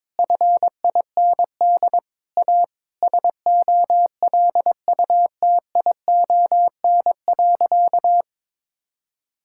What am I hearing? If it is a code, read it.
Morse code, heard as a keyed tone, about 22 wpm, 700 Hz. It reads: FIND A SOLUTION.
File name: practice / 22wpm / Sentences with 8-Word Phrases - 22wpm